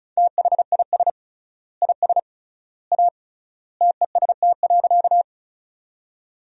Morse code, heard as a keyed tone, about 34 wpm, 700 Hz. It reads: THIS IS A TEST.
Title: 3x Word-Spacing - Sentences from Top 500 Words - 35wpm